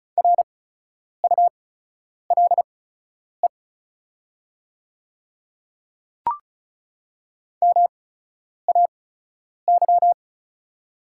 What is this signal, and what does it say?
Morse code, heard as a keyed tone, about 36 wpm, spaced as for 10 wpm, 690 Hz. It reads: RULE MAY
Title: 8x Character Spacing - Top 500 Words - 35wpm